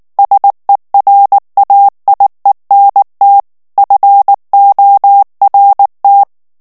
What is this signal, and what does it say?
Morse code, heard as a keyed tone, about 19 wpm, 800 Hz. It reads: SERAIENT FORT
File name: CW / FICHIERS-MP3 / FRANCAIS TEXTE CLAIR 2 VITESSE 18 MOTS MINUTES